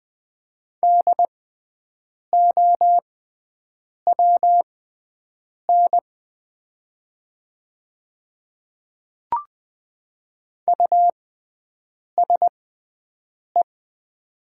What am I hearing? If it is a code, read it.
Morse code, heard as a keyed tone, about 20 wpm, spaced as for 7 wpm, 700 Hz. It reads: DOWN USE
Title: 6x Character Spacing - Top 100 Words - 20wpm